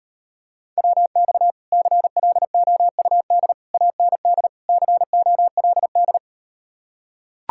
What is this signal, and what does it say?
Morse code, heard as a keyed tone, about 38 wpm, 700 Hz. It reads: WX CLOUD AND COLD